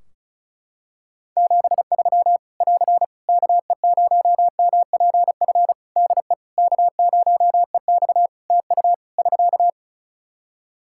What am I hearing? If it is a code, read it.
Morse code, heard as a keyed tone, about 35 wpm, 700 Hz. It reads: 73 <AR> KE0MPF DE K0EX TU <SK>
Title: Practice QSO - Exchange 4 - Ending - 35wpm